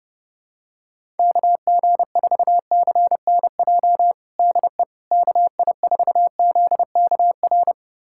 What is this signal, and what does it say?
Morse code, heard as a keyed tone, about 30 wpm, 700 Hz. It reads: KG4CNJ DE KI4ZKR
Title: USA Callsign DE Callsign - 6 Characters - 30wpm